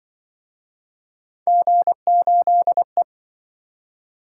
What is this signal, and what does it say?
Morse code, heard as a keyed tone, about 26 wpm, 700 Hz. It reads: G8E